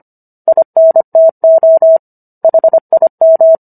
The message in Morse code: INTO HIM

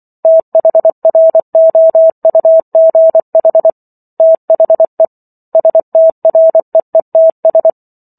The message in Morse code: THROUGH THE STREETS

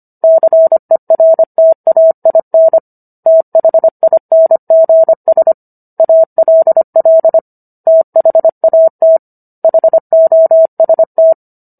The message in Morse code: CERTAIN THINGS ALL THAT HOST

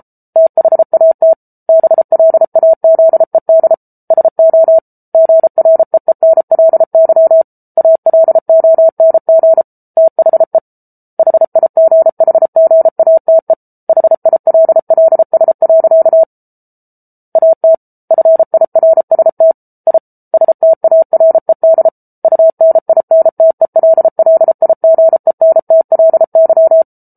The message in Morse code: THAT BLAZED SO GREENLY ALONG THE HIGHGATE HILLS. AT FIRST I STARED UNINTELLIGENTLY

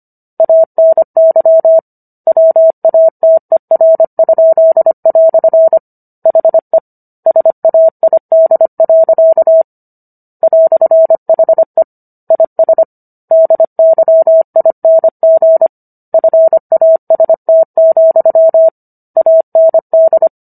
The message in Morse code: ANY WATER?" HE SAID. "HE IS DYING FAST, AND

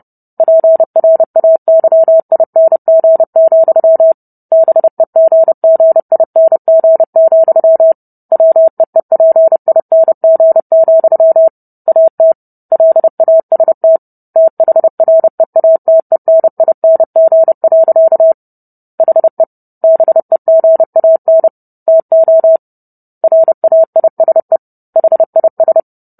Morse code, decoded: PRAYING, BEGGING, WEEPING, AT LAST THREATENING. HE BEGAN TO RAISE HIS

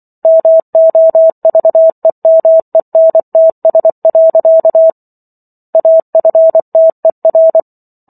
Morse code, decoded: MOVEMENTS. AFTER

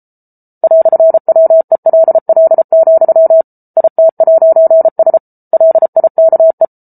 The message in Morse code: "WELL, IT'S LIKE

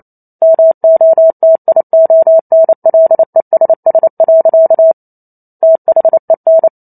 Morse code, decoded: MOTIONLESS. THEN

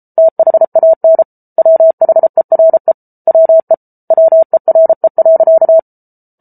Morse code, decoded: THAN WHERE WE WERE.

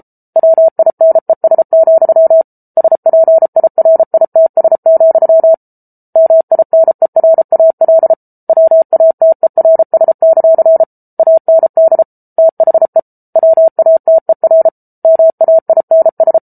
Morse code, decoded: WINES, SPIRITS, MINERAL WATERS; AND THE WATER MAINS

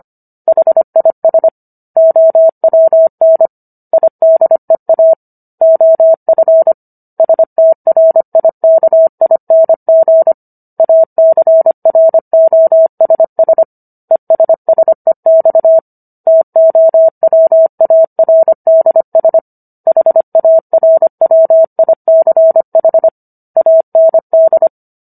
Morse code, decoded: HIS OWN IDEA OF STRIKING ACROSS ESSEX TOWARDS HARWICH AND

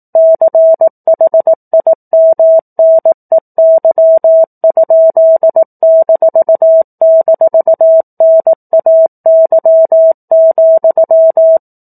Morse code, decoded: CHIMNEY?--NAY,

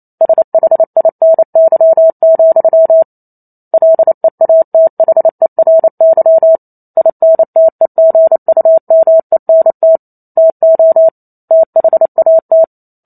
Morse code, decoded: SHINY, LEATHERY INTEGUMENT TO THAT

